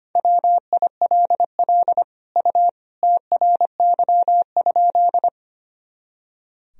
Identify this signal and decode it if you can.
Morse code, heard as a keyed tone, about 25 wpm, 700 Hz. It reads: WILL U TRY?
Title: Sentences from Top 200 Words and Abbreviations 25wpm